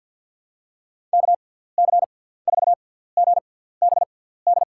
Morse code, heard as a keyed tone, about 47 wpm, spaced as for 17 wpm, 700 Hz. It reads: KC4GBD
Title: USA Call Signs - 6 Characters - 6x Character Spacing - 50wpm